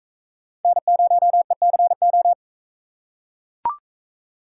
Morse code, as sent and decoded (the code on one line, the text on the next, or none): -. ----- . -.-. ---
N0ECO